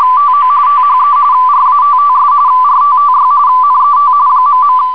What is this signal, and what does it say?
RTTY, about 45.45 baud. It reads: TO THE RATTS RTTY NET ..